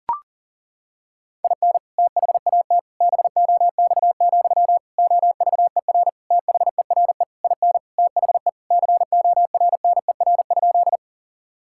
Morse code, heard as a keyed tone, about 40 wpm, 700 Hz. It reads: IN THAT BOX, OVER THERE IN THE CORNER?
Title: Sentences from Top 1600 Words 40wpm